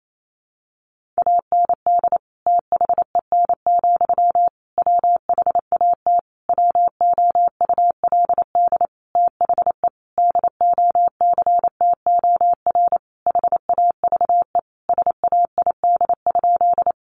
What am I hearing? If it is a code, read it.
Morse code, heard as a keyed tone, about 28 wpm, 700 Hz. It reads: AND THEN, WHAT WOULD THE DOCTOR HAVE SAID?
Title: Sentences from Top 1500 Words 28wpm